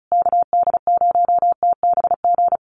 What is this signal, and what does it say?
Morse code, heard as a keyed tone, about 34 wpm, 700 Hz. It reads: KD0TBG
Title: USA Call Signs Repeated 3x 35wpm